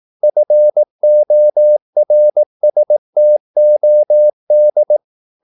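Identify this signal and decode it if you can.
Morse code, heard as a keyed tone, about 18 wpm, 600 Hz. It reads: FORSTOD
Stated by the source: Slow (QRS) Morse practice at 18 wpm, CW med Gnister.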